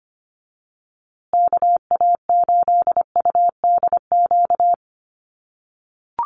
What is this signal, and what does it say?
Morse code, heard as a keyed tone, about 25 wpm, 700 Hz. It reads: KA8UDQ
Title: USA Call Signs Repeated 3x 25wpm